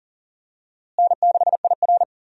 Morse code, heard as a keyed tone, about 40 wpm, 700 Hz. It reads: N6IR
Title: USA Call Signs - 4 Characters - Repeated 2x - 40wpm